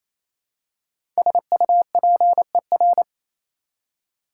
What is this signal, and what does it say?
Morse code, heard as a keyed tone, about 28 wpm, 700 Hz. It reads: SUPER
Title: Prefixes and Suffixes 28wpm